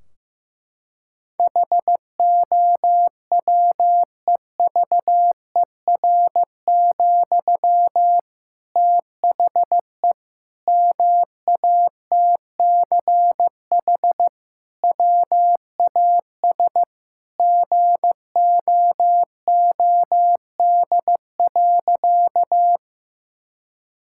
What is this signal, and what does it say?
Morse code, heard as a keyed tone, about 15 wpm, 700 Hz. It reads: HOWEVER, THE MATCH WAS GOOD.
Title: Sentences from Top 1600 Words 15wpm